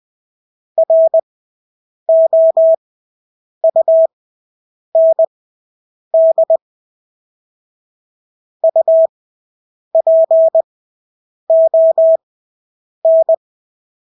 Morse code, heard as a keyed tone, about 20 wpm, spaced as for 8 wpm, 650 Hz. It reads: ROUND UPON
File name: Engelske ord 20/8 wpm